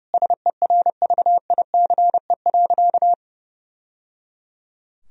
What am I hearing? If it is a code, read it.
Morse code, heard as a keyed tone, about 30 wpm, 700 Hz. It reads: SERVICE.
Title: Sentences from Top 1000 Words - Review Part 4 - 30wpm